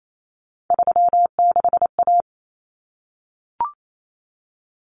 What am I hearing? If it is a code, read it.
Morse code, heard as a keyed tone, about 28 wpm, 700 Hz. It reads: 36A
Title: Common Cut Numbers - 3-Digit 28wpm